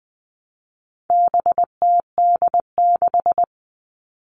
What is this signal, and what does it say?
Morse code, heard as a keyed tone, about 20 wpm, 700 Hz. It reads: BTD6
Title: All Cut Numbers - 4-Digit 20wpm